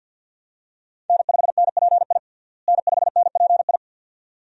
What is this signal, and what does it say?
Morse code, heard as a keyed tone, about 52 wpm, 700 Hz. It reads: N5NPI N5NPI
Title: USA Call Signs - 5 Characters - Repeated 2x - 50wpm